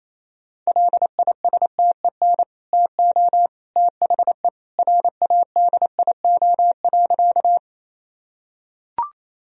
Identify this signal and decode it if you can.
Morse code, heard as a keyed tone, about 27 wpm, 700 Hz. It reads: LISTEN TO THE RADIO.
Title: Sentences from Top 1000 Words - Review Part 3 - 28wpm